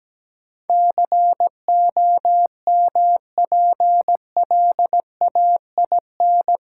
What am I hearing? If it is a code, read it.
Morse code, heard as a keyed tone, about 17 wpm, 700 Hz. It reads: COMPLAIN
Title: Sentences from Top 1300 Words - New Word Drill - 17wpm